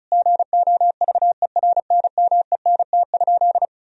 Morse code, heard as a keyed tone, about 34 wpm, 700 Hz. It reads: GOVERNMENT?
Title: Sentences from Top 1000 Words - Review Part 4 - 35wpm